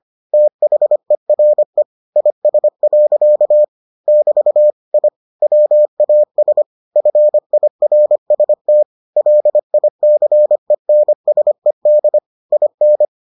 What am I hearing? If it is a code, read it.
Morse code, heard as a keyed tone, about 25 wpm, 600 Hz. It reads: THERE IS. <BT> I WAS FIRST LICENSED IN